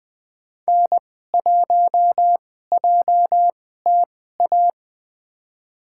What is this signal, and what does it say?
Morse code, heard as a keyed tone, about 20 wpm, spaced as for 15 wpm, 700 Hz. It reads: N1JTA
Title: USA Call Signs - 5 Characters - 2x Character Spacing - 20wpm